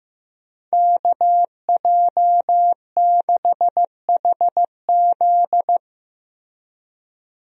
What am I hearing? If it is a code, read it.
Morse code, heard as a keyed tone, about 15 wpm, 700 Hz. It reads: KJ6HZ
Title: USA Call Signs - 5 Characters - Repeated 2x - 15wpm